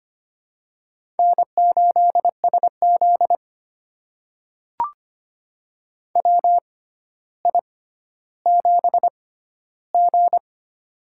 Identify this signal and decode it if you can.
Morse code, heard as a keyed tone, about 25 wpm, spaced as for 13 wpm, 700 Hz. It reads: N8SZ WI7G
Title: USA Call Signs - 4 Characters - 6x Character Spacing - 25wpm